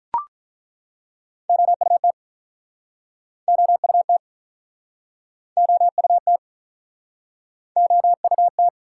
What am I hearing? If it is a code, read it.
Morse code, heard as a keyed tone, about 49 wpm, 700 Hz. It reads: OUT OUT OUT OUT